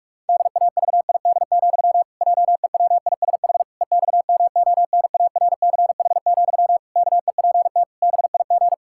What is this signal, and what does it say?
Morse code, heard as a keyed tone, about 41 wpm, 700 Hz. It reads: DAVID, JEWISH EXMONARCH, KEPT BIG
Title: Pangram Challenge v2 - 45wpm